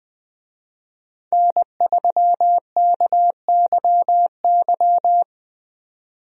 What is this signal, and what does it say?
Morse code, heard as a keyed tone, about 20 wpm, 700 Hz. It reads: N3KYY
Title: USA Call Signs - 5 Characters - 20wpm